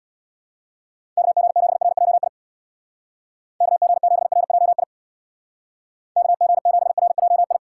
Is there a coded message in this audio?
Yes: Morse code from 1.2 to 7.6 s, heard as a keyed tone, about 65 wpm, 700 Hz. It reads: KG7RJI KG7RJI KG7RJI